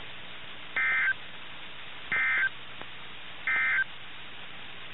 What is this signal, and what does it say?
Emergency Alert System (SAME) tones, heard as harsh buzzing bursts. It is heard from 0.6 to 3.9 s.
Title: NOAA Weather Radio - KIH61